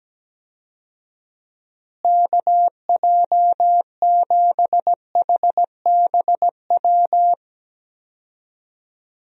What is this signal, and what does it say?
Morse code, heard as a keyed tone, about 17 wpm, 700 Hz. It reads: KJ7HBW
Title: USA Call Signs - 6 Characters - Repeated 3x - 17wpm